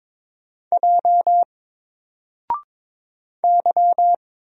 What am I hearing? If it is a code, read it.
Morse code, heard as a keyed tone, about 22 wpm, 700 Hz. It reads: J Y
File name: Single-Letter-Number-22wpm